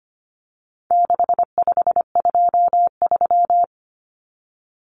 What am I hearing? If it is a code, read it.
Morse code, heard as a keyed tone, about 25 wpm, 700 Hz. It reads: 6523
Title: Numbers - 4-Digit 25wpm